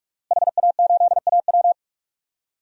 Morse code, heard as a keyed tone, about 44 wpm, 700 Hz. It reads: HA8AW
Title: DXCC-100 Call Signs - 5 Characters - Repeated 3x - 45wpm